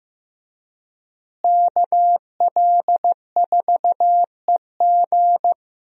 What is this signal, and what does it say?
Morse code, heard as a keyed tone, about 15 wpm, 700 Hz. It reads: KL4EG
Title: USA Call Signs - 5 Characters - Repeated 2x - 15wpm